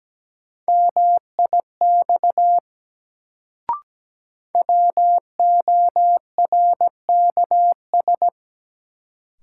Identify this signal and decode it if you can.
Morse code, heard as a keyed tone, about 17 wpm, 700 Hz. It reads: MIX WORKS
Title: Sentences from Top 1100 Words - New Word Drill Alt - 17wpm